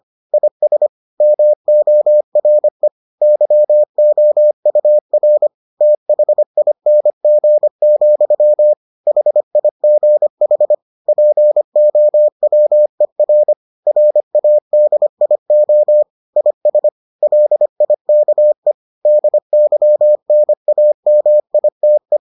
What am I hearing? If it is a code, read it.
Morse code, heard as a keyed tone, about 25 wpm, 600 Hz. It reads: IS MORE YOUR THING, HIGH POWER RADIO IS LIKE DYNAMITE